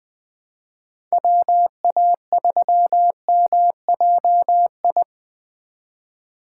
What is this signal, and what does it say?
Morse code, heard as a keyed tone, about 21 wpm, 700 Hz. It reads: WA3MJI